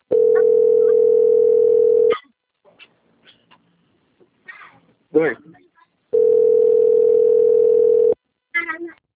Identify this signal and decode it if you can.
Ringback tone (North America), 440 + 480 Hz, with speech.
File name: out- - - - - .7207